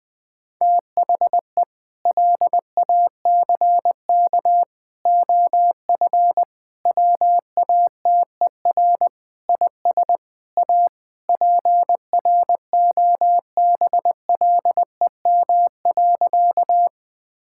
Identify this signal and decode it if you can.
Morse code, heard as a keyed tone, about 21 wpm, 700 Hz. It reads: THE LACK OF WATER IS A PROBLEM.
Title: Sentences with 3-Word Phrases - 20wpm